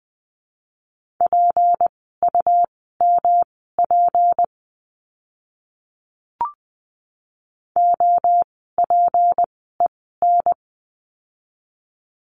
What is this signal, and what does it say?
Morse code, heard as a keyed tone, about 20 wpm, spaced as for 14 wpm, 700 Hz. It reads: PUMP OPEN
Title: 153 Lesson 25f - Letter p - Words 20wpm